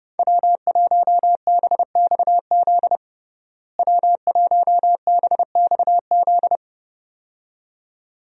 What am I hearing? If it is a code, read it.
Morse code, heard as a keyed tone, about 30 wpm, 700 Hz. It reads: W1BXZ W1BXZ
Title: USA Call Signs - 5 Characters - Repeated 2x - 30wpm